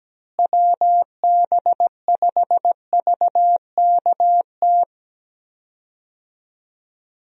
Morse code, heard as a keyed tone, about 17 wpm, 700 Hz. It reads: WB5VKT
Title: USA Call Signs - 6 Characters - Repeated 3x - 17wpm